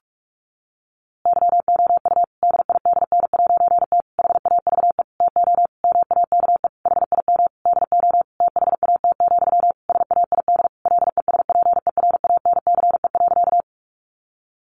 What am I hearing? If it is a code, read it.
Morse code, heard as a keyed tone, about 44 wpm, 700 Hz. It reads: YOU DIDN'T HAVE TO MAKE HIM DO THAT, SAID LESPERANCE.
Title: A Sound Of Thunder 45wpm